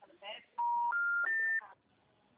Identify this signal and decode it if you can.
Special information tone, with speech.
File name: out- -2012- - - .39634